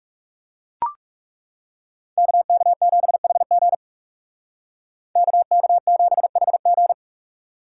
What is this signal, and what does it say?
Morse code, heard as a keyed tone, about 37 wpm, 700 Hz. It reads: KK7HG KK7HG
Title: Speed Racing - USA 2x2 Call Signs 30wpm